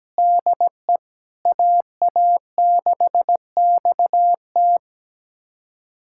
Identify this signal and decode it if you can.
Morse code, heard as a keyed tone, about 17 wpm, 700 Hz. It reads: DE AA6XT
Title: USA Callsign DE Callsign 17wpm